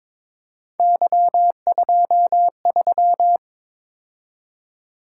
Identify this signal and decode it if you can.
Morse code, heard as a keyed tone, about 26 wpm, 700 Hz. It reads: Y23
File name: Speed-Racing - Three Letter-Number Combinations - 22wpm